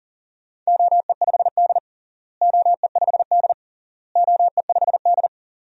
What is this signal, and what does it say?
Morse code, heard as a keyed tone, about 40 wpm, 700 Hz. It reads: OE5D OE5D OE5D
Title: DXCC-100 Call Signs - 4 Characters - Repeated 3x - 40wpm